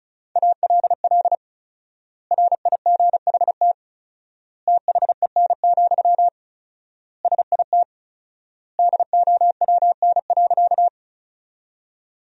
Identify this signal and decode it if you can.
Morse code, heard as a keyed tone, about 36 wpm, 700 Hz. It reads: ALL RIGHT THEN, SIT DOWN.
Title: 4x Word-Spacing - Sentences from Top 500 Words - 35wpm